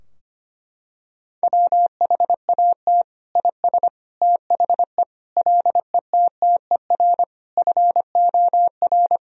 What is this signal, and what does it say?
Morse code, heard as a keyed tone, about 25 wpm, 700 Hz. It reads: WHAT IS THE LETTER FOR
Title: Sentences from Top 1000 Words - Review Part 2 - 25wpm